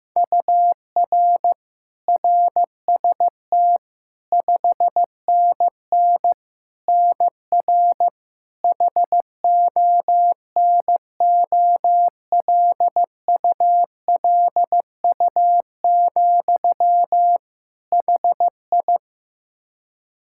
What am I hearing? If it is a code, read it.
Morse code, heard as a keyed tone, about 15 wpm, 700 Hz. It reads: UR RST 5NN NR HONOLULU, HI
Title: Practice QSO - Exchange 1 - Beginning - 15wpm